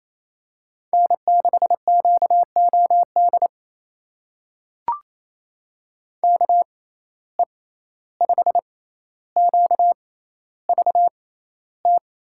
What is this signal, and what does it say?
Morse code, heard as a keyed tone, about 28 wpm, spaced as for 14 wpm, 700 Hz. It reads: N6QOD KE5QVT
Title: USA Call Signs - 6x Character Spacing - 28wpm